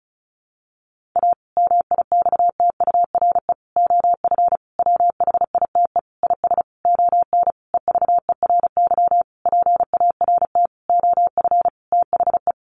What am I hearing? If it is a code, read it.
Morse code, heard as a keyed tone, about 34 wpm, 700 Hz. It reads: A MIXTURE OF WHITE IS ON EVERY PART OF THE
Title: Sentences from Top 1500 Words 35wpm